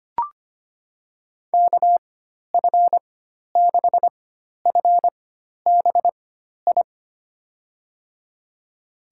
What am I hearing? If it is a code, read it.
Morse code, heard as a keyed tone, about 25 wpm, spaced as for 12 wpm, 700 Hz. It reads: KF6FBI
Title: USA Call Signs - 6 Characters - 4x Character Spacing - 25wpm